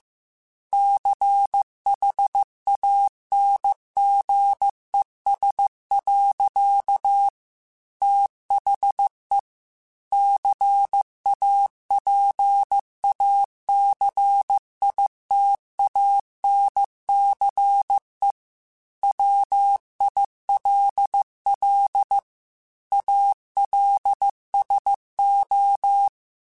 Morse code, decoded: CHANGES. THE CAPACITANCE WILL ALSO

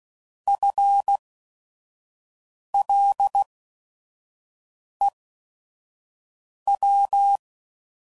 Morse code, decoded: FLEW

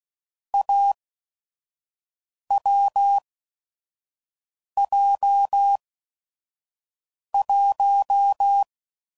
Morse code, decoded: AWJ1

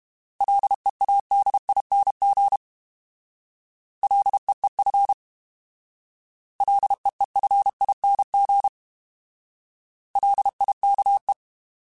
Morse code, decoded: LEADING LEEF LEEFING LIKE

